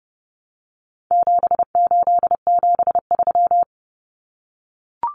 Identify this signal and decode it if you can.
Morse code, heard as a keyed tone, about 30 wpm, 700 Hz. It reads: 7873
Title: Common Cut Numbers - 4-Digit 30wpm